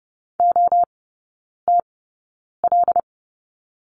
Morse code, heard as a keyed tone, about 30 wpm, spaced as for 9 wpm, 700 Hz. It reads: OTL
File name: Lesson 18e - Letter m - Three Characters 30wpm